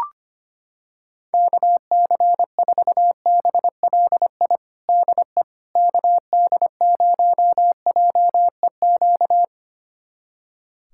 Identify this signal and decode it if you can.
Morse code, heard as a keyed tone, about 25 wpm, 700 Hz. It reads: KC4BLI DE KD0JEQ
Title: USA Callsign DE Callsign 25wpm